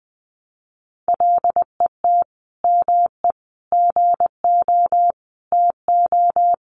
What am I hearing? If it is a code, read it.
Morse code, heard as a keyed tone, about 20 wpm, 700 Hz. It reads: LET ME GO TO
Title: Sentences from Top 300 Words 20wpm